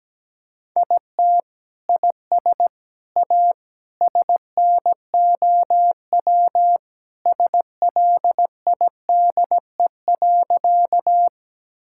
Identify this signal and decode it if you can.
Morse code, heard as a keyed tone, about 17 wpm, 700 Hz. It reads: IT IS A SNOW SLIDE.
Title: Sentences from Top 1600 Words 17wpm